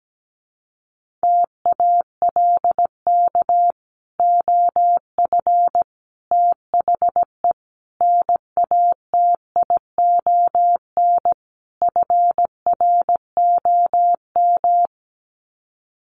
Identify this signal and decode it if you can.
Morse code, heard as a keyed tone, about 17 wpm, 700 Hz. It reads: TALK OF THE NATION FROM
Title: Sets of 5 Words 17wpm